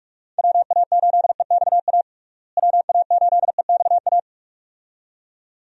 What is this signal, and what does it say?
Morse code, heard as a keyed tone, about 41 wpm, 700 Hz. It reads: WA8EXA WA8EXA
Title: USA Call Signs - 6 Characters - Repeated 2x - 45wpm